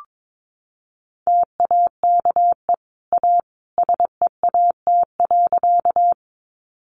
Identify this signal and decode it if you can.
Morse code, heard as a keyed tone, about 22 wpm, 700 Hz. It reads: TAKE A SEAT.
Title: Sentences from Top 1500 Words 22wpm